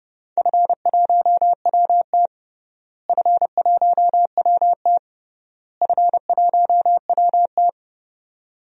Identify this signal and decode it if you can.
Morse code, heard as a keyed tone, about 30 wpm, 700 Hz. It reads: F1WT F1WT F1WT
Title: DXCC-100 Call Signs - 4 Characters - Repeated 3x - 30wpm